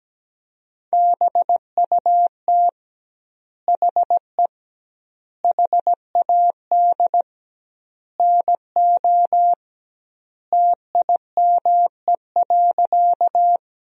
Morse code, heard as a keyed tone, about 17 wpm, 700 Hz. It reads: BUT HE HAD NO TIME.
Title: 2x Word-Spacing - Sentences from Top 100 Words - 17wpm